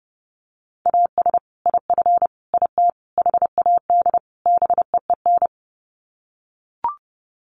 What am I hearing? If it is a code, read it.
Morse code, heard as a keyed tone, about 30 wpm, 700 Hz. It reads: AS IF IT HAD BEEN
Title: Sets of 5 Words Encore2 30wpm